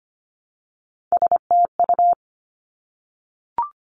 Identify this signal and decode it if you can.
Morse code, heard as a keyed tone, about 25 wpm, 700 Hz. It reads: STU
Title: Three Challenging Letter-Number Combinations 25wpm